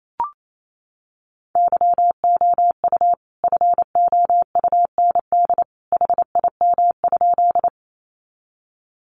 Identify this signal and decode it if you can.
Morse code, heard as a keyed tone, about 28 wpm, 700 Hz. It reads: YOU FOUND HIM?
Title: Sentences from Top 300 Words 28wpm